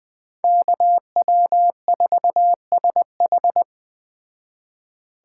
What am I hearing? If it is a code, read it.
Morse code, heard as a keyed tone, about 20 wpm, 700 Hz. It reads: KW4SH
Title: Speed Racing - USA 2x2 Call Signs 17wpm